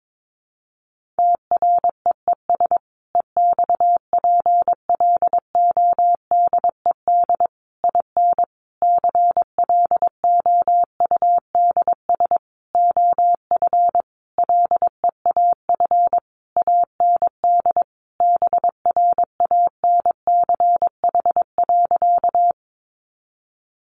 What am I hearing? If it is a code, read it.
Morse code, heard as a keyed tone, about 22 wpm, 700 Hz. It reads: TREES EXPLODED IN CLOUDS OF LEAF AND BRANCH.
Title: A Sound Of Thunder 22wpm